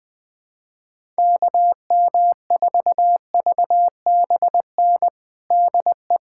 Morse code, heard as a keyed tone, about 20 wpm, 700 Hz. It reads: KM4VBN DE
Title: USA Callsign DE Callsign - 6 Characters - 20wpm